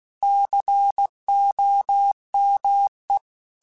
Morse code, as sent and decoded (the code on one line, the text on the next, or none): -.-. --- -- .
COME